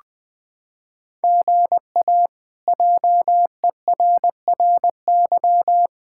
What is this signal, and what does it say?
Morse code, heard as a keyed tone, about 21 wpm, 700 Hz. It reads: GA JERRY
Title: K1USN SST QSO Practice - 20wpm